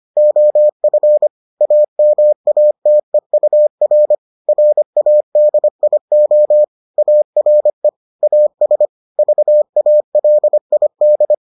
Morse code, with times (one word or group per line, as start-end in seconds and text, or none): 0.2-1.3 OF
1.6-4.2 AMATEUR
4.5-6.6 RADIO
7.0-7.9 ARE
8.2-8.9 AS
9.2-11.3 VALID